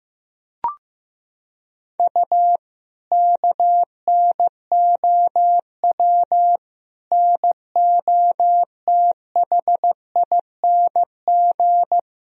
Morse code, heard as a keyed tone, about 15 wpm, 700 Hz. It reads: U KNOW NOTHING
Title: Sentences from Top 500 Words and Abbreviations 15wpm